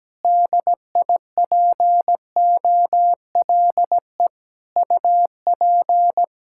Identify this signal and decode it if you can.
Morse code, heard as a keyed tone, about 17 wpm, 700 Hz. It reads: DIPOLE UP
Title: K1USN SST QSO Practice 17wpm